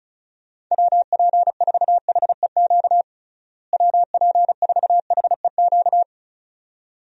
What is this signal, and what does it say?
Morse code, heard as a keyed tone, about 36 wpm, 700 Hz. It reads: WP4HEQ WP4HEQ
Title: USA Call Signs - 6 Characters - Repeated 2x - 35wpm